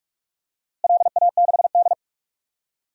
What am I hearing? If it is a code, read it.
Morse code, heard as a keyed tone, about 46 wpm, 700 Hz. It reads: LA6D
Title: DXCC-100 Call Signs - 4 Characters - Repeated 2x - 45wpm